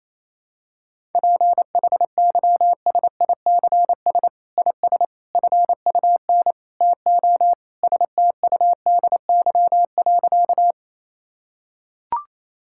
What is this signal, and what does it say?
Morse code, heard as a keyed tone, about 28 wpm, 700 Hz. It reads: PHYSICS IS FUN TO STUDY.
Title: Sentences from Top 1000 Words - Review Part 5 - 28wpm